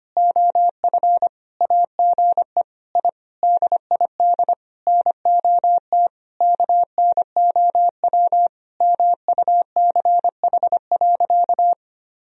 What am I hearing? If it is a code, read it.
Morse code, heard as a keyed tone, about 25 wpm, 700 Hz. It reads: OF AGE I DID NOT KNOW MUCH.
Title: Sentences from Top 1000 Words - Review Part 3 - 25wpm